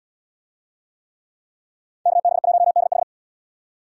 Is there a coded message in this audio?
Yes: Morse code from 2.1 to 3.0 s, heard as a keyed tone, about 77 wpm, 700 Hz. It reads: KB1DH